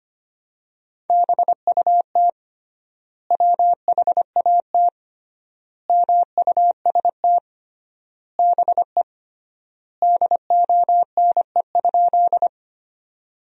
Morse code, heard as a keyed tone, about 25 wpm, 700 Hz. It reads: BUT WHAT MUST BE DONE?
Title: 3x Word-Spacing - Sentences from Top 500 Words - 25wpm